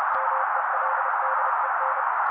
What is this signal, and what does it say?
Morse code, heard as a keyed tone, about 30 wpm, 530 Hz. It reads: MELDEN
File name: daily / 29wpm noise